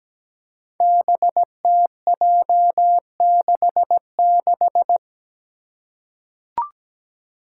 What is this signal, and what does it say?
Morse code, heard as a keyed tone, about 17 wpm, 700 Hz. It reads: BTJ66
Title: Five Challenging Letter-Number Combinations 17wpm